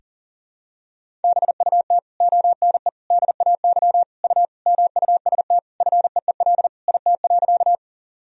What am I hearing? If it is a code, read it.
Morse code, heard as a keyed tone, about 40 wpm, 700 Hz. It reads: BUT ONE DAY U MUST FEEL IT.
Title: Sentences from Top 400 Words and Abbreviations 40wpm